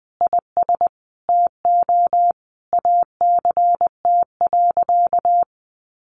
Morse code, heard as a keyed tone, about 20 wpm, 700 Hz. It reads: IS TO ACT.
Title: Sentences from Top 1200 Words 20wpm